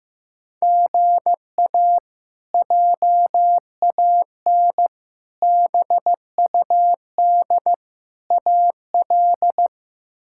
Morse code, heard as a keyed tone, about 15 wpm, 700 Hz. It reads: GA JAN BUD AL
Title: K1USN SST QSO Practice - 15wpm